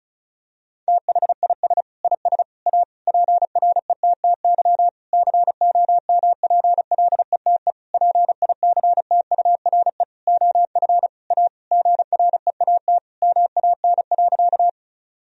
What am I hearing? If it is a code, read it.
Morse code, heard as a keyed tone, about 34 wpm, 700 Hz. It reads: THIS IS A PRETTY COMPLETE PICTURE OF A GREAT MAN.
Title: Sentences from Top 1000 Words - Review Part 2 - 35wpm